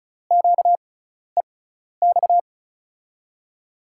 Morse code, heard as a keyed tone, about 36 wpm, spaced as for 12 wpm, 700 Hz. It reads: QEX
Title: USA Call Signs - 6 Characters - 6x Character Spacing - 35wpm